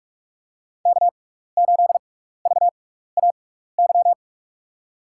Morse code, heard as a keyed tone, about 44 wpm, spaced as for 15 wpm, 700 Hz. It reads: K8VAY
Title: USA Call Signs - 6x Character Spacing - 45wpm